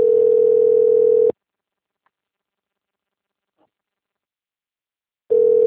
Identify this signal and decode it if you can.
Ringback tone (North America), 440 + 480 Hz.